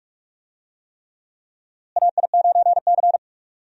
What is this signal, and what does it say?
Morse code, heard as a keyed tone, about 44 wpm, 700 Hz. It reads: AI9C
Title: USA Call Signs - 4 Characters - Repeated 3x - 45wpm